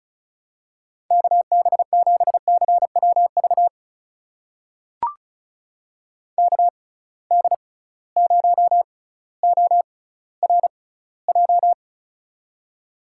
Morse code, heard as a keyed tone, about 36 wpm, spaced as for 18 wpm, 700 Hz. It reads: KB7CWV KD0ORJ
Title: USA Call Signs - 6 Characters - 6x Character Spacing - 35wpm